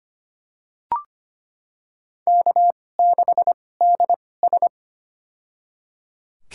Morse code, heard as a keyed tone, about 25 wpm, spaced as for 18 wpm, 700 Hz. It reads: K6DS
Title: USA Call Signs - 4 Characters - 2x Character Spacing - 25wpm